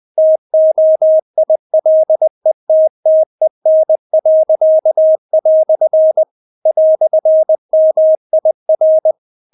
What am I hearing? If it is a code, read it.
Morse code, heard as a keyed tone, about 20 wpm, 630 Hz. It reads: TOILETTEN." "MIR
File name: Witz1 20wpm